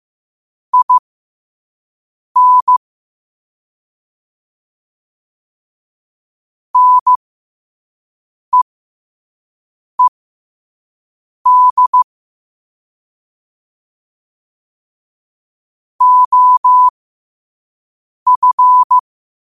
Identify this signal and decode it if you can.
Morse code, heard as a keyed tone, about 15 wpm, spaced as for 5 wpm, 1000 Hz. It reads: IN NEED OF
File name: qotdfiles / 5wpm